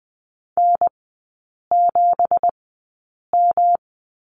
Morse code, heard as a keyed tone, about 20 wpm, spaced as for 8 wpm, 700 Hz. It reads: N7M
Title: 191 Lesson 31e - Letter v - Three Characters 20wpm